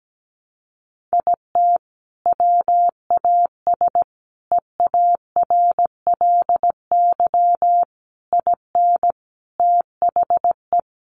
Morse code, heard as a keyed tone, about 17 wpm, 700 Hz. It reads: IT WAS EARLY IN THE
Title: Sentences from Top 1200 Words 17wpm